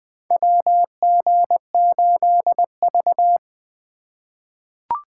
Morse code, heard as a keyed tone, about 20 wpm, 700 Hz. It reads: WG8V
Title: USA Call Signs - 4 Characters - 3x Character Spacing - 20wpm